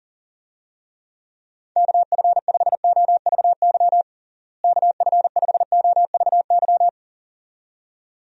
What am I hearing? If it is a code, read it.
Morse code, heard as a keyed tone, about 40 wpm, 700 Hz. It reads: KF5OVY KF5OVY
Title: USA Call Signs - 6 Characters - Repeated 2x - 40wpm